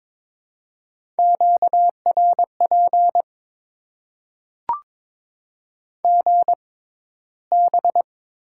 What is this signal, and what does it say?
Morse code, heard as a keyed tone, about 22 wpm, spaced as for 14 wpm, 700 Hz. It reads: QRP GB
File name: Top 90 QSO Elements - 6x Character Spacing - 22wpm